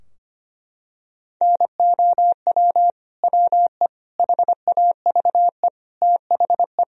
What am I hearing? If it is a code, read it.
Morse code, heard as a keyed tone, about 25 wpm, 700 Hz. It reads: NOW WE HAVE THE
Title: Sentences from Top 1000 Words - Review Part 3 - 25wpm